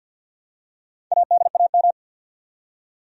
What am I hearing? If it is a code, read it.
Morse code, heard as a keyed tone, about 52 wpm, 700 Hz. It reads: ADAM
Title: USA Common Names - 50wpm